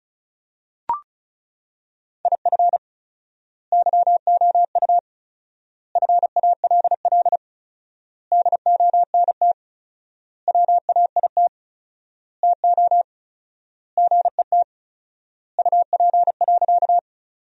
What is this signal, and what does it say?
Morse code, heard as a keyed tone, about 36 wpm, 700 Hz. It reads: IF YOU FALL DONT WAIT TO GET UP.
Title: 4x Word-Spacing - Sentences from Top 500 Words - 35wpm